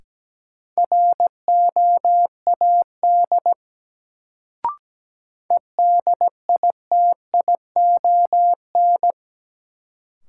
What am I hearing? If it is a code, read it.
Morse code, heard as a keyed tone, about 17 wpm, 700 Hz. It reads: ROAD EDITION
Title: Sets of 5 Words - New Word Drill - 17wpm